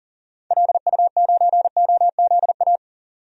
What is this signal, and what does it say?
Morse code, heard as a keyed tone, about 40 wpm, 700 Hz. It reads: LU9OZA
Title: DXCC-100 Call Signs - 6 Characters - 40wpm